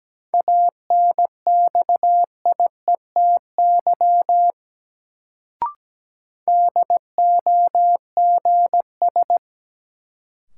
Morse code, heard as a keyed tone, about 17 wpm, 700 Hz. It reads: ANXIETY DOGS
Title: Sentences from Top 1600 Words - New Word Drill Alt - 17wpm